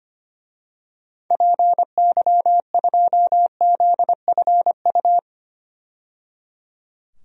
Morse code, heard as a keyed tone, about 25 wpm, 700 Hz. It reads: PY2ZFU
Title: DXCC-100 Call Signs - 6 Characters - 25wpm